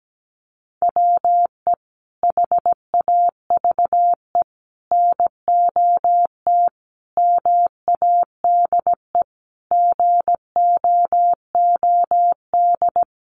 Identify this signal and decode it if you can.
Morse code, heard as a keyed tone, about 17 wpm, 700 Hz. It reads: WE HAVE NOT MADE GOOD
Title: Sentences from Top 1400 Words 17wpm